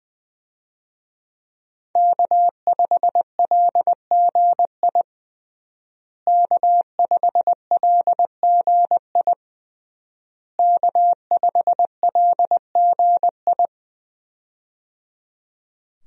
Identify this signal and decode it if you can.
Morse code, heard as a keyed tone, about 20 wpm, 700 Hz. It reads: K5LGI K5LGI K5LGI
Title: USA Call Signs - 5 Characters - Repeated 3x - 20wpm